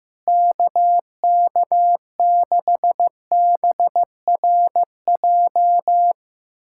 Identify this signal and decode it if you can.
Morse code, heard as a keyed tone, about 14 wpm, 700 Hz. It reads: KK6BRJ
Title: USA Callsign DE Callsign - 6 Characters - 15wpm